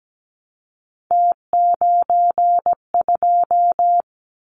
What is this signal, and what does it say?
Morse code, heard as a keyed tone, about 17 wpm, 700 Hz. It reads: T92
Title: Callsign Prefixes for Top 100 DXCC - 17wpm